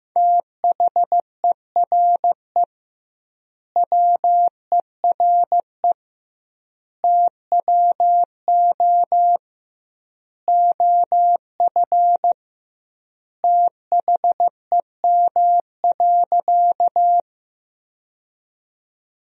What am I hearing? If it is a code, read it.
Morse code, heard as a keyed tone, about 15 wpm, 700 Hz. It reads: THERE WERE TWO OF THEM.
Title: 2x Word-Spacing - Sentences from Top 100 Words - 15wpm